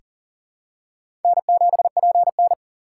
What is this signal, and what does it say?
Morse code, heard as a keyed tone, about 37 wpm, 700 Hz. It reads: N7PN
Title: USA Call Signs - 4 Characters - 4x Character Spacing - 40wpm